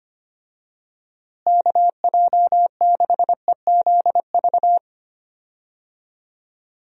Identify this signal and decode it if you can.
Morse code, heard as a keyed tone, about 25 wpm, 700 Hz. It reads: KJ6EZV
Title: USA Call Signs - 6 Characters - Repeated 2x - 25wpm